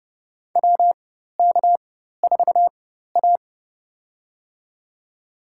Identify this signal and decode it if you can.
Morse code, heard as a keyed tone, about 30 wpm, spaced as for 14 wpm, 700 Hz. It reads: WK4A
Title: USA Call Signs - 4 Characters - 4x Character Spacing - 30wpm